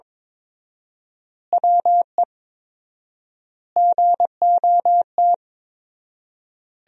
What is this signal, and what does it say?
Morse code, heard as a keyed tone, about 22 wpm, 700 Hz. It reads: WE GOT
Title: 4x Word-Spacing - Sentences from Top 500 Words - 22wpm